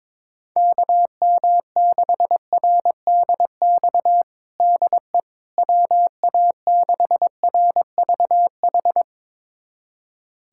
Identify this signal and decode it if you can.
Morse code, heard as a keyed tone, about 22 wpm, 700 Hz. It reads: KM6RDX DE WA6RVH
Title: USA Callsign DE Callsign - 6 Characters - 22wpm